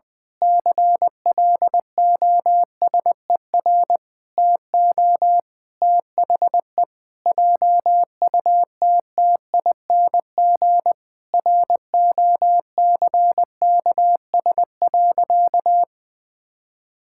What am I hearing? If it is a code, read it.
Morse code, heard as a keyed tone, about 20 wpm, 700 Hz. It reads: CLOSER TO THE JUTTING ROCKS.
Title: Pangram Challenge v2 - 20wpm